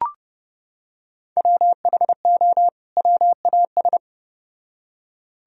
Morse code, heard as a keyed tone, about 29 wpm, 700 Hz. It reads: WHO WAS